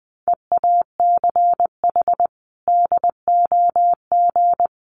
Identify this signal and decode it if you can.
Morse code, heard as a keyed tone, about 20 wpm, 700 Hz. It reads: EACH DOG
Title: Sentences from Top 1300 Words 20wpm